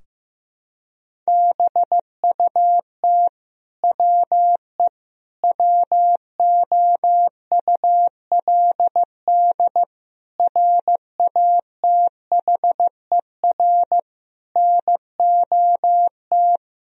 Morse code, heard as a keyed tone, about 15 wpm, 700 Hz. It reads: BUT WE WOULD RATHER NOT
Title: Sentences from Top 1000 Words - Review Part 4 - 15wpm